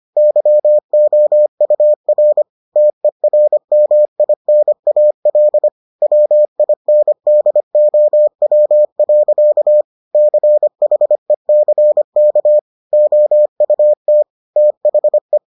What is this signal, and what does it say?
Morse code, heard as a keyed tone, about 25 wpm, 600 Hz. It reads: YOUR TERMINAL WINDOW. CHECK OUT THE